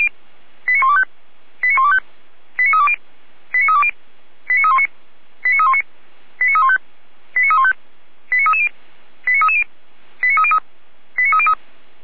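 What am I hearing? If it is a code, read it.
Five-tone selective call (ZVEI), heard as quick runs of five tones, tones of about 80 ms. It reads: 89125 89125 89320 89320 89319 89319 89315 89315 89330 89330 89303 89303
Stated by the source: Five-tone ZVEI selective call received on an SDR.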